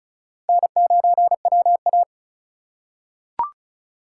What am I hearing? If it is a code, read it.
Morse code, heard as a keyed tone, about 36 wpm, 700 Hz. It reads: N9WA
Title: USA Call Signs - 4 Characters - 2x Character Spacing - 35wpm